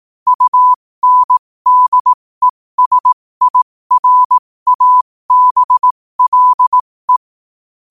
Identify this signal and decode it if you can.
Morse code, heard as a keyed tone, about 18 wpm, spaced as for 15 wpm, 1000 Hz. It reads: UNDESIRABLE
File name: qotdfiles / 15wpm